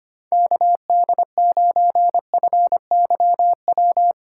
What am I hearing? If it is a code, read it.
Morse code, heard as a keyed tone, about 25 wpm, 700 Hz. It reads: KD9FYW
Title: USA Call Signs - 6 Characters - 6x Character Spacing - 25wpm